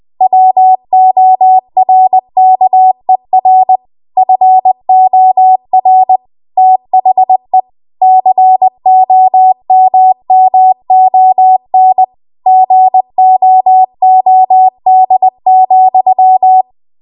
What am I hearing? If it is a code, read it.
Morse code, heard as a keyed tone, about 20 wpm, 750 Hz. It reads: WORKER FOR THE COMMON GOOD,
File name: Archive / 20 WPM / 20WPM